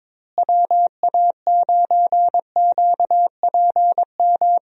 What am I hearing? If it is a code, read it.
Morse code, heard as a keyed tone, about 22 wpm, 700 Hz. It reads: WA9QPM
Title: USA Call Signs - 6 Characters - Repeated 3x - 22wpm